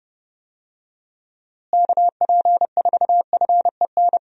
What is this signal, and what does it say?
Morse code, heard as a keyed tone, about 31 wpm, 700 Hz. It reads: KP4FEN
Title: DXCC-100 Call Signs - 6 Characters - Repeated 2x - 30wpm